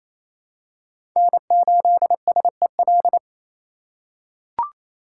Morse code, heard as a keyed tone, about 28 wpm, 700 Hz. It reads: N8SEL
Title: USA Call Signs - 5 Characters - 4x Character Spacing - 28wpm